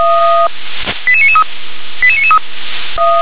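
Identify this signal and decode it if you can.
Five-tone selective call (ZVEI), heard as quick runs of five tones, tones of about 70 ms. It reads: 88993 88993
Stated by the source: Five-tone ZVEI selective call received on SDR.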